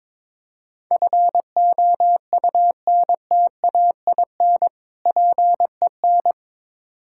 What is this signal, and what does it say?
Morse code, heard as a keyed tone, about 22 wpm, 700 Hz. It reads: FOUNTAIN PEN
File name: Difficult Words with Hints - 22wpm